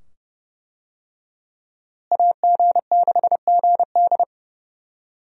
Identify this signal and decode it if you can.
Morse code, heard as a keyed tone, about 32 wpm, 700 Hz. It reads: AG6GD
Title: USA Call Signs - 5 Characters - Repeated 2x - 30wpm